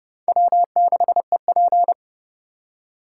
Morse code, heard as a keyed tone, about 28 wpm, 700 Hz. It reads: W6EP